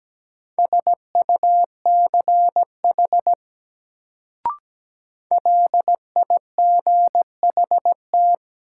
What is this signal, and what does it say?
Morse code, heard as a keyed tone, about 17 wpm, 700 Hz. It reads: SUCH LIGHT
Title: Sentences from Top 200 Words - New Word Drill - 17wpm